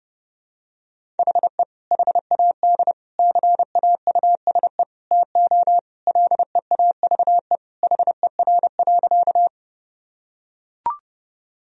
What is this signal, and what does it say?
Morse code, heard as a keyed tone, about 30 wpm, 700 Hz. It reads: HE HAD CAUSE TO LEAVE HER.
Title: Sentences from Top 1000 Words - Review Part 1 - 30wpm